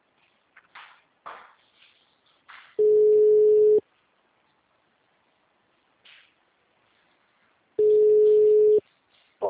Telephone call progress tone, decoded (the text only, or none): ringback tone